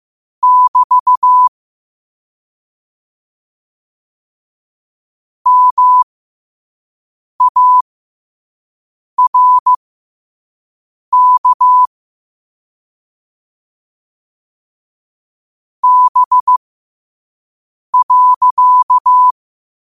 Morse code, heard as a keyed tone, about 15 wpm, spaced as for 5 wpm, 1000 Hz. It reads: <BT> MARK B.
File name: qotdfiles / 5wpm